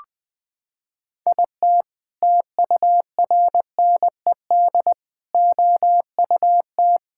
Morse code, heard as a keyed tone, about 20 wpm, 700 Hz. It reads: IT TURNED OUT
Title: Sets of 6 Words - 20wpm